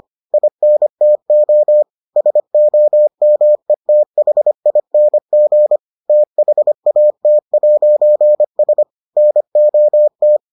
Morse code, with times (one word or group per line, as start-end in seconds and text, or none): 0.3-1.8 INTO
2.2-5.8 SOMETHING
6.1-8.8 THAT'S
9.2-10.4 NOT